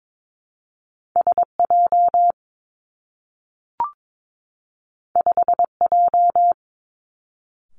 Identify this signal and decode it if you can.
Morse code, heard as a keyed tone, about 22 wpm, 700 Hz. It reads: S J 5 J
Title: Two Challenging Letter-Number Combinations 22wpm